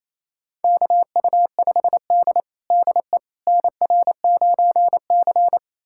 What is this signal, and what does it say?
Morse code, heard as a keyed tone, about 29 wpm, 690 Hz. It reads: KU5D DE NR9C
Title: USA Callsign DE Callsign - 4 Characters - 28wpm